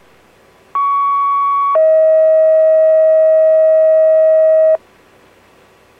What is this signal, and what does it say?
Two-tone page, heard as a one-second tone, then a three-second tone, about 1120 then 640 Hz. It is heard from 0.7 to 4.8 s.